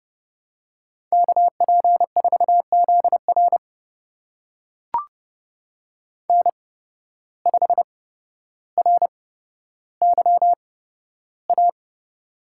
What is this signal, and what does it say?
Morse code, heard as a keyed tone, about 31 wpm, spaced as for 13 wpm, 700 Hz. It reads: KP4ZR N5RYA
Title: USA Call Signs - 8x Character Spacing - 30wpm